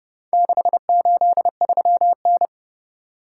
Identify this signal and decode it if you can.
Morse code, heard as a keyed tone, about 29 wpm, 700 Hz. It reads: 683N